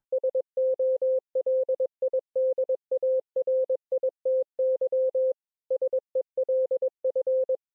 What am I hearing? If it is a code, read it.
Morse code, heard as a keyed tone, about 22 wpm, 520 Hz. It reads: SOLIDARITY SELF